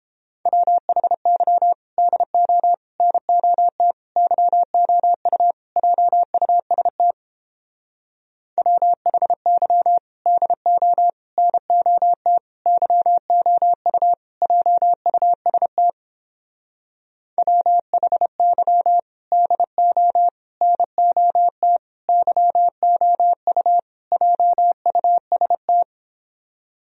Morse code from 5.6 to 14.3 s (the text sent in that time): JUST WHY DO NOT YOU